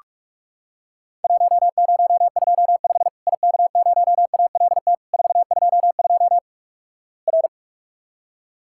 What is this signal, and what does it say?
Morse code, heard as a keyed tone, about 48 wpm, 700 Hz. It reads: 1025 IK0ALT 422 R
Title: Worked All Europe DX Contest - QTC Exchange - 45wpm